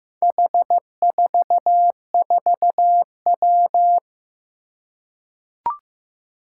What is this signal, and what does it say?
Morse code, heard as a keyed tone, about 15 wpm, 700 Hz. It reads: H44W
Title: Four Challenging Letter-Number Combinations 15wpm